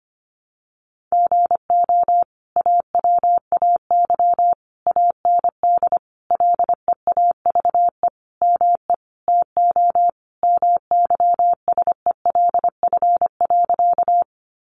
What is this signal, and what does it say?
Morse code, heard as a keyed tone, about 25 wpm, 700 Hz. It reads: GO AWAY AND LEAVE ME TO MYSELF.
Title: Sentences from Top 1300 Words 25wpm